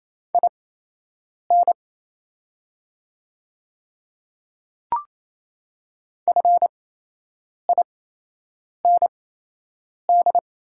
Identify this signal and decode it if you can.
Morse code, heard as a keyed tone, about 27 wpm, spaced as for 8 wpm, 690 Hz. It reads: IN FIND
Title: 8x Character Spacing - Top 200 Words - 28wpm